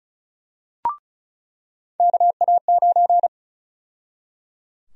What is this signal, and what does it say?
Morse code, heard as a keyed tone, about 36 wpm, 700 Hz. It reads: KA9
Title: USA Call Sign Prefixes - 35wpm